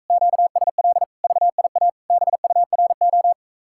Morse code, heard as a keyed tone, about 44 wpm, 700 Hz. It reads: QSL VIA BURO